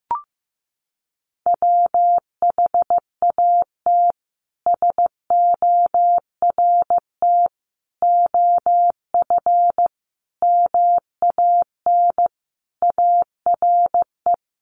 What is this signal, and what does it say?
Morse code, heard as a keyed tone, about 15 wpm, 700 Hz. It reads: WHAT SORT OF MAN ARE
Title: Sentences from Top 1300 Words 15wpm